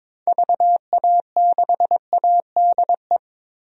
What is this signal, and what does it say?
Morse code, heard as a keyed tone, about 22 wpm, 700 Hz. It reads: VA6ADE
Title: DXCC-100 Call Signs - 6 Characters - Repeated 2x - 22wpm